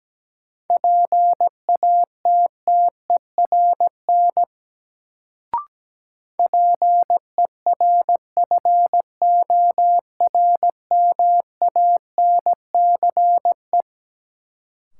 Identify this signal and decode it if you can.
Morse code, heard as a keyed tone, about 17 wpm, 700 Hz. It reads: PATTERN PERFORMANCE
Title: Sentences from Top 1000 Words - New Word Drill - 17wpm